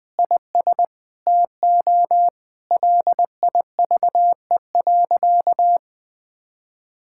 Morse code, heard as a keyed tone, about 20 wpm, 700 Hz. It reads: IS TO LIVE.
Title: Sentences from Top 1000 Words - Review Part 5 - 20wpm